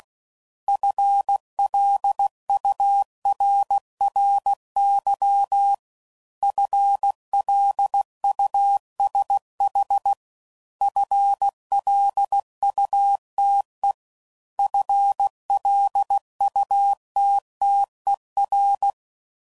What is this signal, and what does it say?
Morse code, heard as a keyed tone, about 16 wpm, 800 Hz. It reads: FLURRY FLUSH FLUTE FLUTTER